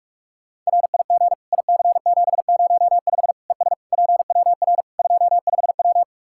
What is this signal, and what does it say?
Morse code, heard as a keyed tone, about 41 wpm, 700 Hz. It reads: RIG IC705 ES PWR 25W
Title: Practice QSO - Exchange 2 - Info - 45wpm